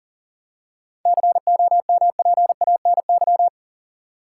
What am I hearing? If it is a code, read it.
Morse code, heard as a keyed tone, about 40 wpm, 700 Hz. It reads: COMPANY
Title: Seven Letter Words - 40wpm